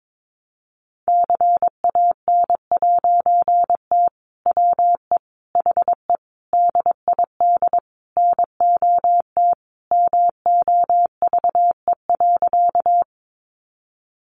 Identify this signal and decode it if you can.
Morse code, heard as a keyed tone, about 22 wpm, 700 Hz. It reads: CAN'T WE HE DID NOT MOVE.
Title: A Sound Of Thunder 22wpm